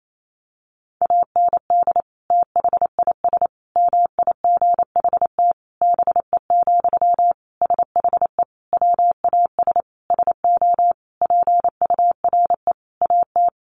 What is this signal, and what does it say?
Morse code, heard as a keyed tone, about 28 wpm, 700 Hz. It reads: AND THIS MIGHT BE, SHE WAS SO PURE AT
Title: Sentences from Top 1500 Words 28wpm